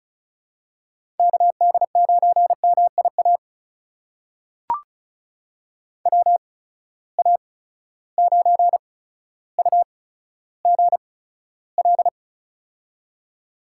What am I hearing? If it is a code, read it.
Morse code, heard as a keyed tone, about 36 wpm, spaced as for 15 wpm, 700 Hz. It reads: KD9MIA WA9UGL
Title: USA Call Signs - 6 Characters - 8x Character Spacing - 35wpm